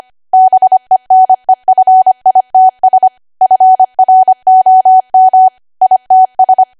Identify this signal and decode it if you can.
Morse code, heard as a keyed tone, about 25 wpm, 750 Hz. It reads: BENEFITS FROM ITS
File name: Archive / 25 WPM / 25WPM